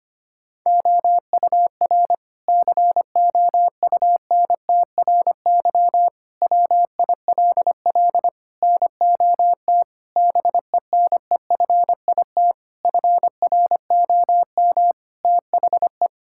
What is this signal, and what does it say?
Morse code, heard as a keyed tone, about 25 wpm, 700 Hz. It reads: OUR COUNTRY WILL NOT BENEFIT FROM THE
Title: Sentences from Top 1000 Words - Review Part 6 - 25wpm